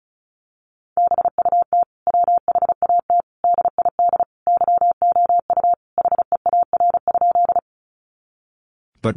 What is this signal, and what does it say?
Morse code, heard as a keyed tone, about 34 wpm, 700 Hz. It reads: BUT WHAT DID YOU HEAR?
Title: Sentences from Top 300 Words 35wpm